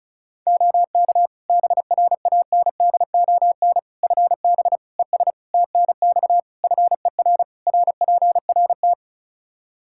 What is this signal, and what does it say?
Morse code, heard as a keyed tone, about 36 wpm, 700 Hz. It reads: OK BRANDON FB ES TNX FER RPRT
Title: Practice QSO - Exchange 2 - Info - 35wpm